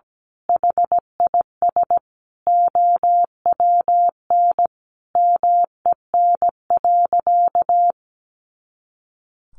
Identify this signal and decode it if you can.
Morse code, heard as a keyed tone, about 17 wpm, 700 Hz. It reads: HIS OWN MEN.
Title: Sentences from Top 300 Words 17wpm